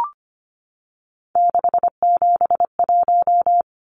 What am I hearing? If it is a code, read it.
Morse code, heard as a keyed tone, about 25 wpm, 700 Hz. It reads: 671
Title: Numbers - 3-Digit 25wpm